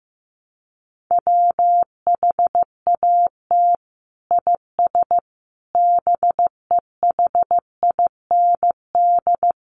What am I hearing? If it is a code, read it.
Morse code, heard as a keyed tone, about 15 wpm, 700 Hz. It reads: WHAT IS BEHIND
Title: Sentences from Top 1100 Words 15wpm